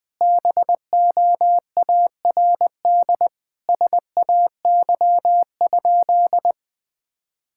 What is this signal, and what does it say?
Morse code, heard as a keyed tone, about 21 wpm, 700 Hz. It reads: BOARD SAY?
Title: Sentences from Top 1000 Words - Review Part 5 - 20wpm